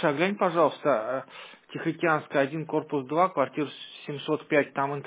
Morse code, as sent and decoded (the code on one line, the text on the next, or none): none